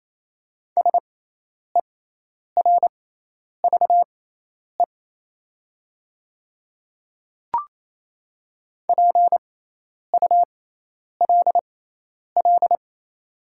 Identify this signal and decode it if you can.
Morse code, heard as a keyed tone, about 28 wpm, spaced as for 10 wpm, 700 Hz. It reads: SERVE PULL
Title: 6x Character Spacing - Top 500 Words - 28wpm